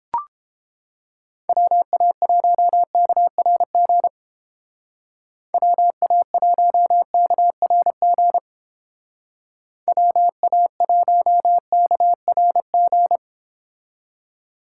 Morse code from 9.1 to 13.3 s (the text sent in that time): WA1KRG